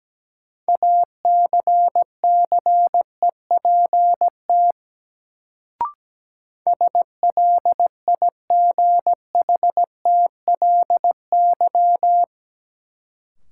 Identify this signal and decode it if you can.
Morse code, heard as a keyed tone, about 17 wpm, 700 Hz. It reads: ACCEPT SLIGHTLY
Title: Sentences from Top 1300 Words - New Word Drill Alt - 17wpm